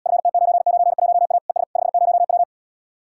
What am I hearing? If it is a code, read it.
Morse code, heard as a keyed tone, about 80 wpm, 700 Hz. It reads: FT991A ES 50W